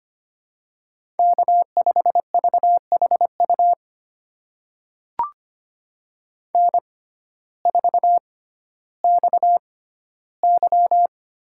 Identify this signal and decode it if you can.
Morse code, heard as a keyed tone, about 25 wpm, spaced as for 14 wpm, 700 Hz. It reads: K5VHU N4XY
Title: USA Call Signs - 5 Characters - 6x Character Spacing - 25wpm